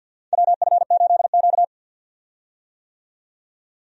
Morse code, heard as a keyed tone, about 49 wpm, 700 Hz. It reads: WF8Q
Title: USA Call Signs - 4 Characters - 50wpm